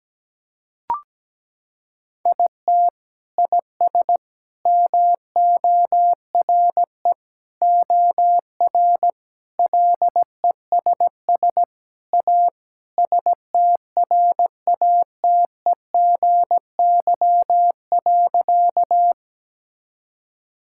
Morse code, heard as a keyed tone, about 17 wpm, 700 Hz. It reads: IT IS MORE OR LESS A STRATEGY.
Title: Sentences from Top 1000 Words - Review Part 5 - 17wpm